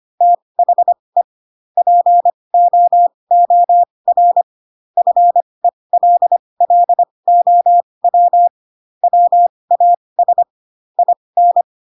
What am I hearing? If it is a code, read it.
Morse code, heard as a keyed tone, about 25 wpm, spaced as for 20 wpm, 700 Hz. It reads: THE POOR FELLOW WAS IN
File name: Pinocchio / Pinocchio ch8 15 25 ik1hge